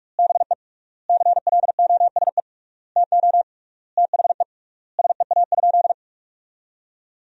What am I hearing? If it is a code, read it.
Morse code, heard as a keyed tone, about 46 wpm, 700 Hz. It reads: BE CLOSE TO THE SEA?
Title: 3x Word-Spacing - Sentences from Top 300 Words - 45wpm